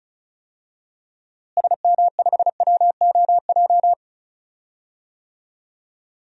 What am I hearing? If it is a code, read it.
Morse code, heard as a keyed tone, about 34 wpm, 700 Hz. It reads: SM5WOJ
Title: DXCC-100 Call Signs - 6 Characters - Repeated 3x - 35wpm